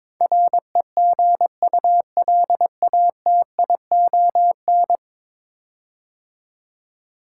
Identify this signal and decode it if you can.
Morse code, heard as a keyed tone, about 22 wpm, 700 Hz. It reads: REGULATION
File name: practice / 22wpm / 10 Letter Words - First 3 Letters Given - 22wpm